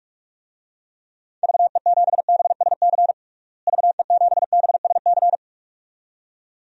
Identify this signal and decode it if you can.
Morse code, heard as a keyed tone, about 46 wpm, 700 Hz. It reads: VE7BSC VE7BSC
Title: DXCC-100 Call Signs - 6 Characters - Repeated 2x - 45wpm